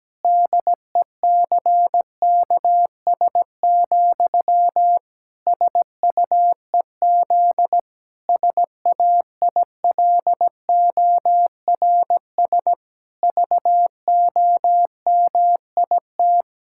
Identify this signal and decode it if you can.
Morse code, heard as a keyed tone, about 17 wpm, 700 Hz. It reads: DECKS, SUEZ SAILORS VOMIT
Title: Pangram Challenge v2 - 17wpm